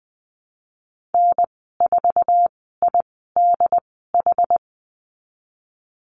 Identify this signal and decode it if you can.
Morse code, heard as a keyed tone, about 20 wpm, spaced as for 15 wpm, 700 Hz. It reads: N4IDH
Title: 193 Lesson 31g - Letter v - Call Signs 20wpm